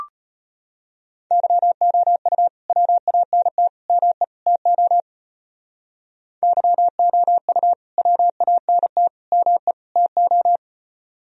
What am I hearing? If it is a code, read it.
Morse code, heard as a keyed tone, about 32 wpm, 700 Hz. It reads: YOU WANT ME TO YOU WANT ME TO